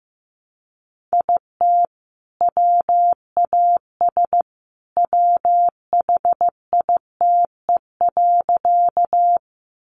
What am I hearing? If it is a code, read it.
Morse code, heard as a keyed tone, about 15 wpm, 700 Hz. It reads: IT WAS WHITE.
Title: Sentences from Top 300 Words 15wpm